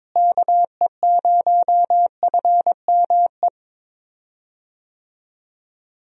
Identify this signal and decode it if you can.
Morse code, heard as a keyed tone, about 21 wpm, 700 Hz. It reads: KE0FME
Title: USA Call Signs - 6 Characters - Repeated 3x - 22wpm